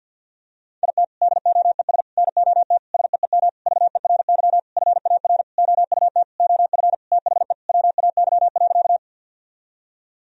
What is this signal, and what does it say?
Morse code, heard as a keyed tone, about 52 wpm, 700 Hz. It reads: IT DOES NOT SEEM VERY FAR OUT OF THE WAY.
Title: Sentences from Top 1000 Words - Review Part 1 - 50wpm